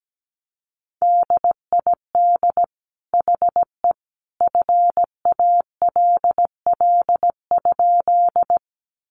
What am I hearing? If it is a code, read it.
Morse code, heard as a keyed tone, about 17 wpm, 700 Hz. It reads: DID HE FALL?
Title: Sentences from Top 500 Words 17wpm